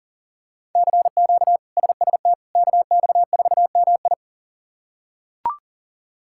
Morse code, heard as a keyed tone, about 40 wpm, 700 Hz. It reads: CQ SST KX4MI
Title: K1USN SST QSO Practice - 40wpm